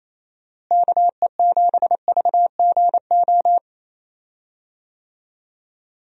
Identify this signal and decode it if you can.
Morse code, heard as a keyed tone, about 28 wpm, 700 Hz. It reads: KE7VGO
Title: USA Call Signs - 6 Characters - Repeated 3x - 28wpm